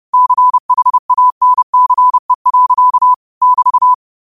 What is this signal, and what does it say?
Morse code, heard as a keyed tone, about 30 wpm, 1000 Hz. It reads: CHANCE. <BT>